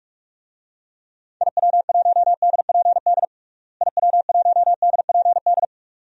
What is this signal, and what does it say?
Morse code, heard as a keyed tone, about 41 wpm, 700 Hz. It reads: IW1DPD IW1DPD
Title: DXCC-100 Call Signs - 6 Characters - Repeated 2x - 45wpm